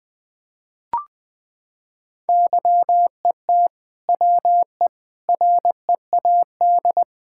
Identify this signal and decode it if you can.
Morse code, heard as a keyed tone, about 20 wpm, 700 Hz. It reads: YET WE READ
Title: Sentences from Top 1600 Words 20wpm